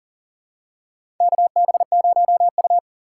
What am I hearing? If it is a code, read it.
Morse code, heard as a keyed tone, about 40 wpm, 700 Hz. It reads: KB0U
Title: USA Call Signs - 2x Character Spacing - 40wpm